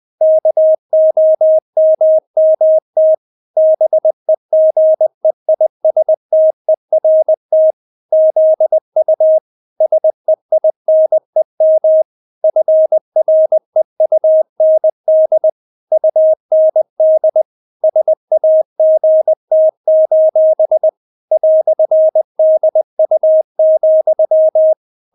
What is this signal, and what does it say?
Morse code, heard as a keyed tone, about 20 wpm, 630 Hz. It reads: KOMMT BEGEISTERT ZU SEINEM FREUND UND SAGT: "DU,
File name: Witz1 20wpm